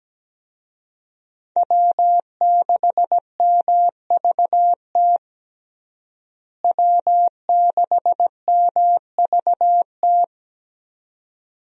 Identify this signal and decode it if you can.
Morse code, heard as a keyed tone, about 17 wpm, 700 Hz. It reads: W6MVT W6MVT
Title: USA Call Signs - 5 Characters - Repeated 2x - 17wpm